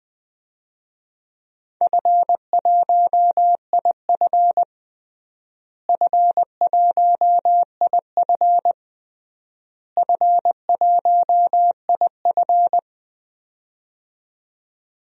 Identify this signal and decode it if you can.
Morse code, heard as a keyed tone, about 20 wpm, 700 Hz. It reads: F1IF F1IF F1IF
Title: DXCC-100 Call Signs - 4 Characters - Repeated 3x - 20wpm